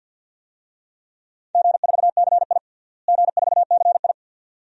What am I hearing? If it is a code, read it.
Morse code, heard as a keyed tone, about 52 wpm, 700 Hz. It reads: G4CI G4CI
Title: DXCC-100 Call Signs - 4 Characters - Repeated 2x - 50wpm